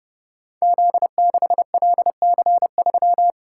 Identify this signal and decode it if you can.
Morse code, heard as a keyed tone, about 30 wpm, 700 Hz. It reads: Z6LC3
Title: Code Groups - 5 Letter:Numbers - 30wpm